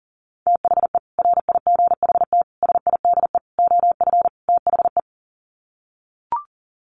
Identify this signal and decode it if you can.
Morse code, heard as a keyed tone, about 40 wpm, 700 Hz. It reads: THE RIGHT SIDE OF THE
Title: Sets of 5 Words Encore 40wpm